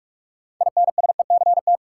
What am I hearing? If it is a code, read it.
Morse code, heard as a keyed tone, about 48 wpm, 700 Hz. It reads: INSECT
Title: Six Letter Words - 45wpm